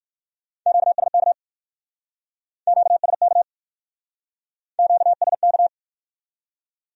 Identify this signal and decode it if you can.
Morse code, heard as a keyed tone, about 51 wpm, 700 Hz. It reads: QSK QSK QSK
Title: Speed-Racing Q-Codes - 45wpm